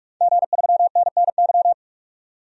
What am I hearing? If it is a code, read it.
Morse code, heard as a keyed tone, about 46 wpm, 700 Hz. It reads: G3NNY
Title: DXCC-100 Call Signs - 5 Characters - 45wpm